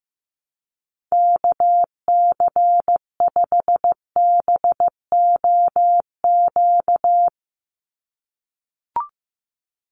Morse code, heard as a keyed tone, about 15 wpm, 700 Hz. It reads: KC5BOQ
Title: USA Call Signs 15wpm